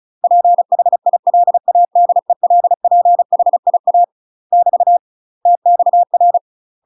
Morse code, heard as a keyed tone, about 36 wpm, 700 Hz. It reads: PHILADELPHIA <BT> TXR
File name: UK MASTER QSOs TRAINING / 35WPM / UK MASTER QSOs 0013